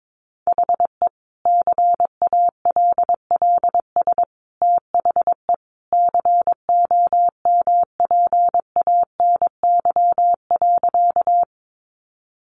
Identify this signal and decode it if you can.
Morse code, heard as a keyed tone, about 22 wpm, 700 Hz. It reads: HE CALLS THE COMPANY.
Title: Additional Easy Phrases 22wpm